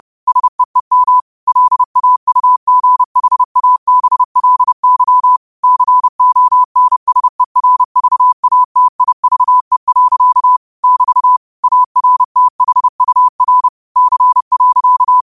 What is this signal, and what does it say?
Morse code, heard as a keyed tone, about 30 wpm, 1000 Hz. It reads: SEEM LAUGHABLY CONSERVATIVE. <BT> ARTHUR C.
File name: 30wpm